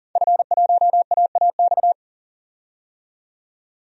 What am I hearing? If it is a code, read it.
Morse code, heard as a keyed tone, about 40 wpm, 700 Hz. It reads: F1AAX
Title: DXCC-100 Call Signs - 5 Characters - 40wpm